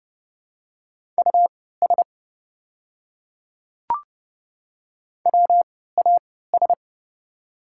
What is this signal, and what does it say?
Morse code, heard as a keyed tone, about 30 wpm, spaced as for 17 wpm, 700 Hz. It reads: US WAS
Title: 3x Character Spacing - Top 300 Words - 30wpm